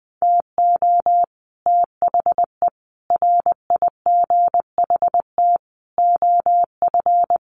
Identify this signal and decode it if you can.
Morse code, heard as a keyed tone, about 20 wpm, 700 Hz. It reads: TO THE RIGHT OF
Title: Sets of 5 Words Encore 20wpm